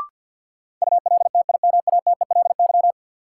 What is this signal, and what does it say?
Morse code, heard as a keyed tone, about 52 wpm, 700 Hz. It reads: ULTIMATELY
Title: Sentences from Top 1200 Words - New Word Drill Alt - 50wpm